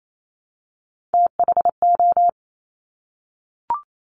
Morse code, heard as a keyed tone, about 28 wpm, 700 Hz. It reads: THO
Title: Top Three Letter Combinations in Words 28wpm